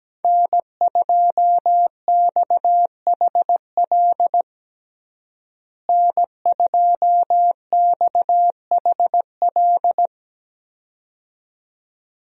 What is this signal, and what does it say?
Morse code, heard as a keyed tone, about 17 wpm, 700 Hz. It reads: N2XHL N2XHL
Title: USA Call Signs - 5 Characters - Repeated 2x - 17wpm